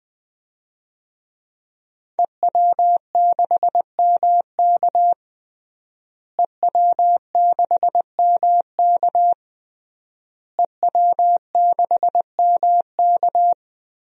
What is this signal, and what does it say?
Morse code, heard as a keyed tone, about 21 wpm, 700 Hz. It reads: EW6MK EW6MK EW6MK
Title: DXCC-100 Call Signs - 5 Characters - Repeated 3x - 20wpm